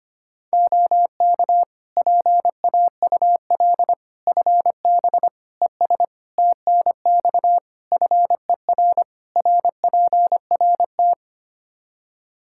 Morse code, heard as a keyed tone, about 25 wpm, 700 Hz. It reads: OK PAUL FB ES TNX FER RPRT
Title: K1USN SST QSO Practice 25wpm